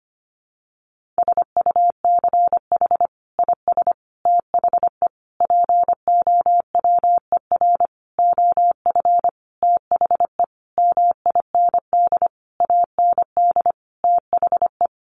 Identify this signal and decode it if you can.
Morse code, heard as a keyed tone, about 25 wpm, 700 Hz. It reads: SUCH IS THE POWER OF THE MIND AND THE
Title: Sentences from Top 1400 Words 25wpm